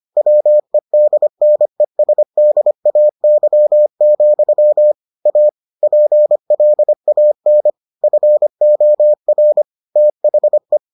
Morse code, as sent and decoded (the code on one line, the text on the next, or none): .-- . -.. -. ... -.. .- -.-- --..-- / .- / .--. .-.. .- -. / ..-. --- .-. / - .... .
WEDNESDAY, A PLAN FOR THE